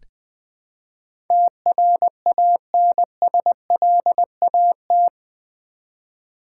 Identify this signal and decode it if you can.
Morse code, heard as a keyed tone, about 20 wpm, 700 Hz. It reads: TRANSLAT
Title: Build A Word - 4 Letters At A Time - 11-Letter Words - 20wpm